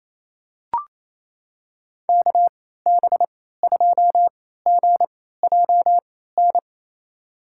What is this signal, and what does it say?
Morse code, heard as a keyed tone, about 29 wpm, spaced as for 16 wpm, 700 Hz. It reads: KB2GJN
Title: USA Call Signs - 3x Character Spacing - 28wpm